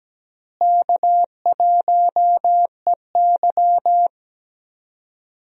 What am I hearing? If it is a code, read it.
Morse code, heard as a keyed tone, about 17 wpm, 700 Hz. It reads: K1EY
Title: USA Call Signs - 4 Characters - Repeated 2x - 17wpm